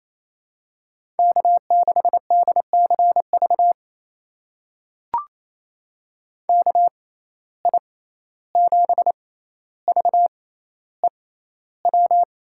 Morse code, heard as a keyed tone, about 28 wpm, spaced as for 14 wpm, 700 Hz. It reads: K6DCV KI7VEW
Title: USA Call Signs - 6x Character Spacing - 28wpm